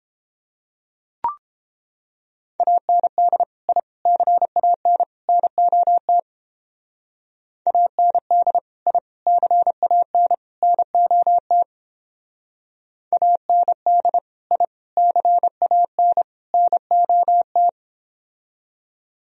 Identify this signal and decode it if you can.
Morse code, heard as a keyed tone, about 28 wpm, 700 Hz. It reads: AND I CAN NOT AND I CAN NOT AND I CAN NOT